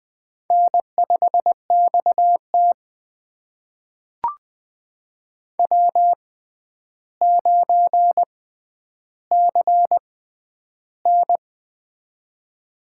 Morse code, heard as a keyed tone, about 20 wpm, spaced as for 10 wpm, 700 Hz. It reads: N5XT W9CN
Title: USA Call Signs - 4 Characters - 6x Character Spacing - 20wpm